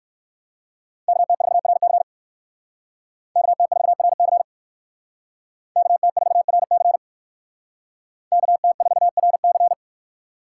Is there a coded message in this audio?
Yes: Morse code from 1.1 to 9.7 s, heard as a keyed tone, about 57 wpm, 700 Hz. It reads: KT4RC KT4RC KT4RC KT4RC